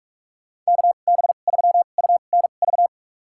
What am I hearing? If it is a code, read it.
Morse code, heard as a keyed tone, about 42 wpm, spaced as for 31 wpm, 700 Hz. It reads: KB3UNV
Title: USA Call Signs - 2x Character Spacing - 45wpm